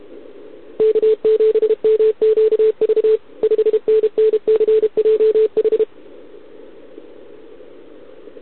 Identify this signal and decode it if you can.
Morse code, heard as a keyed tone, about 32 wpm, 440 Hz. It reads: K7MQV 5NNCJH
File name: ARCHIV / OKDX2020 / OK1RIb